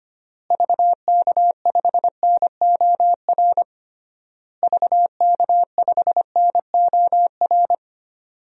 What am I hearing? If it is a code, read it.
Morse code, heard as a keyed tone, about 25 wpm, 700 Hz. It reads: VK5NOR VK5NOR
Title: DXCC-100 Call Signs - 6 Characters - Repeated 3x - 25wpm